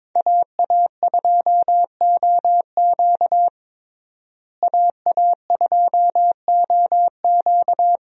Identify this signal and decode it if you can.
Morse code, heard as a keyed tone, about 22 wpm, 700 Hz. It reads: AA2OQ AA2OQ
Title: USA Call Signs - 5 Characters - Repeated 2x - 22wpm